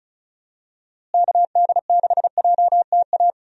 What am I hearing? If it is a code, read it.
Morse code, heard as a keyed tone, about 34 wpm, 700 Hz. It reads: KD6JTA
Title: USA Call Signs - 6 Characters - Repeated 3x - 35wpm